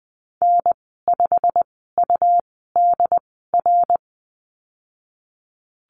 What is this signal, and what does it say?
Morse code, heard as a keyed tone, about 19 wpm, spaced as for 14 wpm, 700 Hz. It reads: N5UDR
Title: 179 Lesson 29g - Slash - Call Signs 20wpm